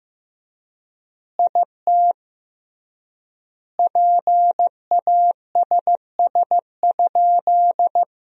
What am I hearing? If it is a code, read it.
Morse code, heard as a keyed tone, about 15 wpm, 700 Hz. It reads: IT PASS?
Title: 3x Word-Spacing - Sentences from Top 400 Words - 15wpm